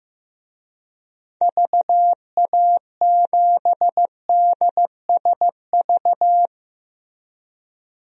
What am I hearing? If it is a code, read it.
Morse code, heard as a keyed tone, about 15 wpm, 700 Hz. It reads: VA7DSV
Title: DXCC-100 Call Signs - 6 Characters - Repeated 3x - 15wpm